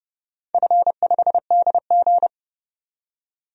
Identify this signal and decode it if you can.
Morse code, heard as a keyed tone, about 30 wpm, 700 Hz. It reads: F5DG
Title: DXCC-100 Call Signs - 4 Characters - 30wpm